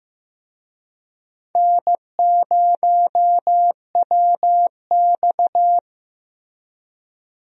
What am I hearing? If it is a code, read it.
Morse code, heard as a keyed tone, about 15 wpm, 700 Hz. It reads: N0WX
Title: USA Call Signs - 4 Characters - Repeated 3x - 15wpm